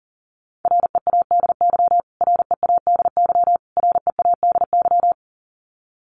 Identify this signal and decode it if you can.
Morse code, heard as a keyed tone, about 40 wpm, 700 Hz. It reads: READY READY READY
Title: Top 500 Words Repeated 40wpm